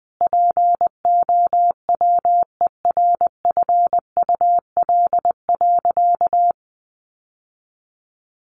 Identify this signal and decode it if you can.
Morse code, heard as a keyed tone, about 20 wpm, 700 Hz. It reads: POWERFUL.
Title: Sentences from Top 1300 Words 20wpm